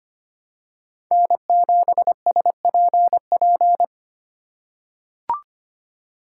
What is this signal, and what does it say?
Morse code, heard as a keyed tone, about 25 wpm, 700 Hz. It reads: N7SPP
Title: USA Call Signs - 2x Character Spacing - 25wpm